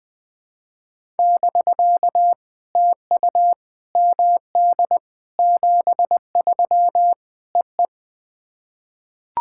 Morse code, heard as a keyed tone, about 21 wpm, 700 Hz. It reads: <BK> TU MD 73 EE
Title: POTA QSO Practice - 20wpm